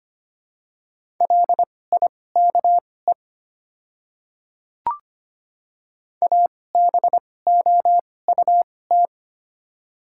Morse code, heard as a keyed tone, about 25 wpm, spaced as for 18 wpm, 700 Hz. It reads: LIKE ABOUT
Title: 2x Character Spacing - Top 100 Words - 25wpm